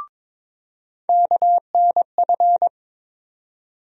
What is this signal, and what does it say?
Morse code, heard as a keyed tone, about 23 wpm, 700 Hz. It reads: KNF
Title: USA Call Sign Postfixes - 3 Letters 22wpm